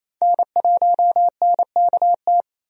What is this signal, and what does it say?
Morse code, heard as a keyed tone, about 27 wpm, 700 Hz. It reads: N1NKT
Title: USA Call Signs - 5 Characters - 28wpm